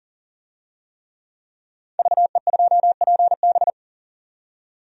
Morse code, heard as a keyed tone, about 40 wpm, 700 Hz. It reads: VE2PB
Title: DXCC-100 Call Signs - 5 Characters - 40wpm